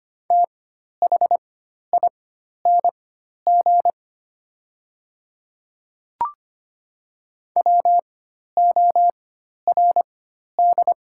Morse code, heard as a keyed tone, about 25 wpm, spaced as for 12 wpm, 700 Hz. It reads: THING WORD
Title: 4x Character Spacing - Top 100 Words - 25wpm